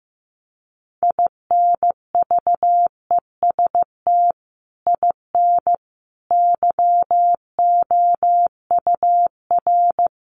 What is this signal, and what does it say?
Morse code, heard as a keyed tone, about 15 wpm, 700 Hz. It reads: INVEST IN YOUR
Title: Sentences from Top 1100 Words 15wpm